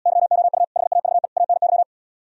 Morse code, heard as a keyed tone, about 78 wpm, 700 Hz. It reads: YOU HAVE ANY